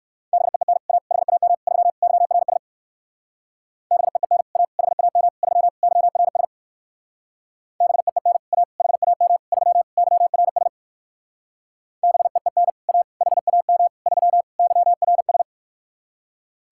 Morse code, heard as a keyed tone, about 58 wpm, 700 Hz. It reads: BEEN A HAM 3 YRS BEEN A HAM 3 YRS BEEN A HAM 3 YRS BEEN A HAM 3 YRS